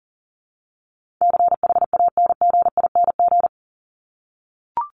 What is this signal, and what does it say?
Morse code, heard as a keyed tone, about 39 wpm, 700 Hz. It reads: CHANGING
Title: ING Suffix - Top 100 Words with Base Words 40wpm